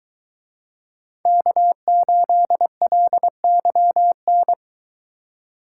Morse code, heard as a keyed tone, about 23 wpm, 700 Hz. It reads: K8LYN